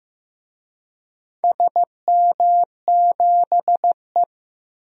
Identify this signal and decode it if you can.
Morse code, heard as a keyed tone, about 15 wpm, 700 Hz. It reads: SM7E
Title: DXCC-100 Call Signs - 4 Characters - Repeated 3x - 15wpm